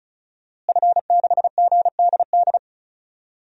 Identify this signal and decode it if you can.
Morse code, heard as a keyed tone, about 36 wpm, 700 Hz. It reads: F6GDD
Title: DXCC-100 Call Signs - 5 Characters - Repeated 3x - 35wpm